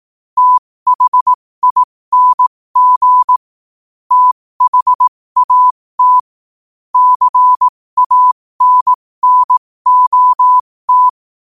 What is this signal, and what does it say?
Morse code, heard as a keyed tone, about 18 wpm, spaced as for 15 wpm, 1000 Hz. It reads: THING THAT CANNOT
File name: qotdfiles / 15wpm